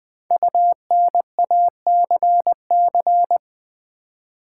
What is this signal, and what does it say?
Morse code, heard as a keyed tone, about 20 wpm, 700 Hz. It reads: UNACC
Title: Build A Word - 5 Letters At A Time - 12-Letter Words - 20wpm